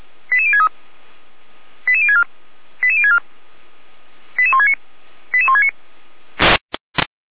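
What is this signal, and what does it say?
Five-tone selective call (ZVEI), heard as quick runs of five tones, tones of about 70 ms. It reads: 88063 88064 88064 88169 88169
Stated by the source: ZVEI five-tone selective call received on SDR.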